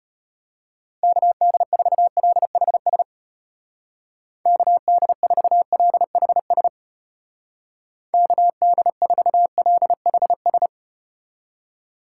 Morse code, heard as a keyed tone, about 32 wpm, 700 Hz. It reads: KD4LHS KD4LHS KD4LHS